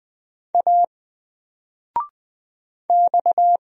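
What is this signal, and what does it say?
Morse code, heard as a keyed tone, about 20 wpm, 700 Hz. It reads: A X
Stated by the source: Single letter and number Morse drill at 20 wpm, from N4LCD.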